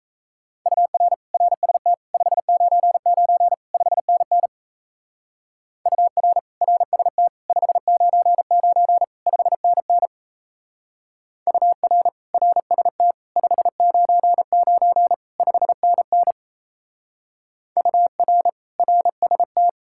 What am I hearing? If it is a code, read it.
Morse code, heard as a keyed tone, about 35 wpm, 700 Hz. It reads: UR RST 599 5NN UR RST 599 5NN UR RST 599 5NN UR RST
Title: Speed-Racing Common QSO Phrases - 28wpm